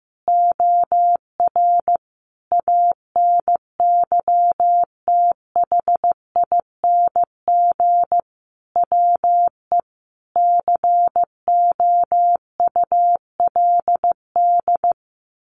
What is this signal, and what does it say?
Morse code, heard as a keyed tone, about 15 wpm, 700 Hz. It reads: OR ANYTHING WE COULD
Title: Sentences from Top 1400 Words 15wpm